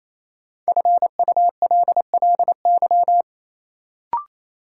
Morse code, heard as a keyed tone, about 28 wpm, 700 Hz. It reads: FULLY
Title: Sentences from Top 1300 Words - New Word Drill - 28wpm